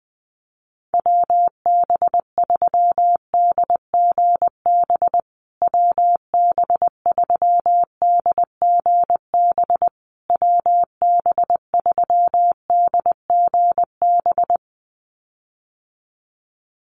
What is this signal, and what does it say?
Morse code, heard as a keyed tone, about 20 wpm, 700 Hz. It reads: WB3DGB WB3DGB WB3DGB
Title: USA Call Signs Repeated 3x 20wpm